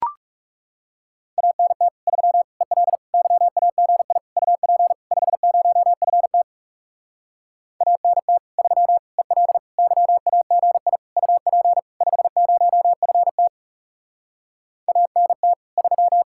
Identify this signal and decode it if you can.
Morse code, heard as a keyed tone, about 40 wpm, 700 Hz. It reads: ANT 3 EL YAGI UP 50FT ANT 3 EL YAGI UP 50FT ANT 3